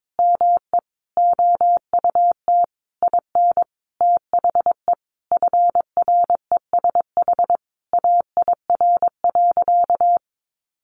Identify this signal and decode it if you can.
Morse code, heard as a keyed tone, about 22 wpm, 700 Hz. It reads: ME OUT IN THE FRESH AIR.
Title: Sentences from Top 1200 Words 22wpm